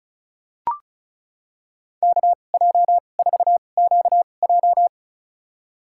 Morse code, heard as a keyed tone, about 34 wpm, spaced as for 25 wpm, 700 Hz. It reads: KJ4QJ
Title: USA Call Signs - 5 Characters - 2x Character Spacing - 35wpm